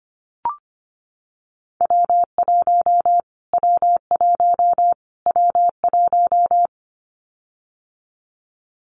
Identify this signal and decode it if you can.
Morse code, heard as a keyed tone, about 25 wpm, 700 Hz. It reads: W1 W1 W1
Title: Callsign Prefixes for Top 25 DXCC - Repeated 3x - 25wpm